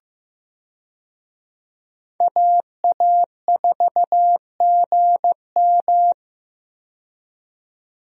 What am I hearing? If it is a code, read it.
Morse code, heard as a keyed tone, about 15 wpm, 700 Hz. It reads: AA4GM
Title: USA Call Signs - 5 Characters - 15wpm